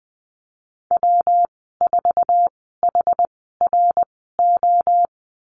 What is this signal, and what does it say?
Morse code, heard as a keyed tone, about 20 wpm, spaced as for 14 wpm, 700 Hz. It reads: W4HRO
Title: 147 Lesson 24g - Letter y - Call Signs 20wpm